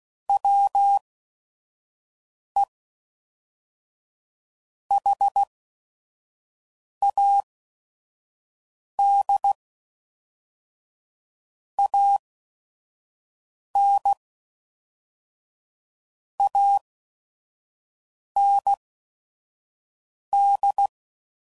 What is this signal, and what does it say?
Morse code, heard as a keyed tone, about 16 wpm, spaced as for 5 wpm, 800 Hz. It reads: WE HAD AN AND
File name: CW Code Groups @5WPM / Technical Subjects / AND Gate